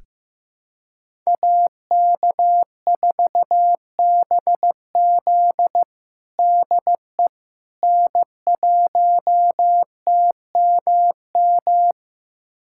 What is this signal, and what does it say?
Morse code, heard as a keyed tone, about 15 wpm, 700 Hz. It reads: AK4BZ DE N1TMM
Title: USA Callsign DE Callsign - 5 Characters - 15wpm